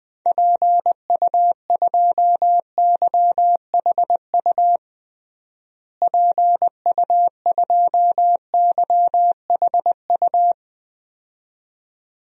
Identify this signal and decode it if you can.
Morse code, heard as a keyed tone, about 20 wpm, 700 Hz. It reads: PU2YHU PU2YHU
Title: DXCC-100 Call Signs - 6 Characters - Repeated 2x - 20wpm